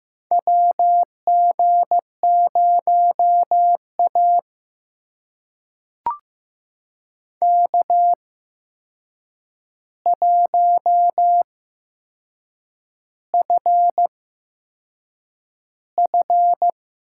Morse code, heard as a keyed tone, about 15 wpm, spaced as for 6 wpm, 700 Hz. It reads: WG0A K1FF
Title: USA Call Signs - 4 Characters - 8x Character Spacing - 15wpm